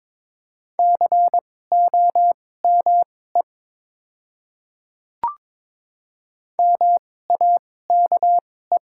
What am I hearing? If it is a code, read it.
Morse code, heard as a keyed tone, about 21 wpm, spaced as for 16 wpm, 700 Hz. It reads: COME MAKE
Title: 2x Character Spacing - Top 200 Words - 22wpm